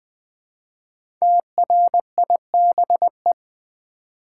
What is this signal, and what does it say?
Morse code, heard as a keyed tone, about 20 wpm, 700 Hz. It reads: TRIBE
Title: Build A Word - 1 Letter At A Time - 5-Letter Words - 20wpm